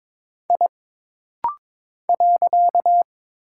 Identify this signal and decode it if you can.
Morse code, heard as a keyed tone, about 22 wpm, 700 Hz. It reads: I .
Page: This rapid-fire timing is geared to push your ICR (Instant Character Recognition) skills.